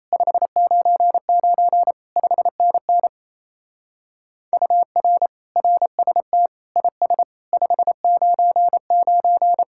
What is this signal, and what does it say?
Morse code, heard as a keyed tone, about 37 wpm, 700 Hz. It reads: 599 5NN UR RST IS 599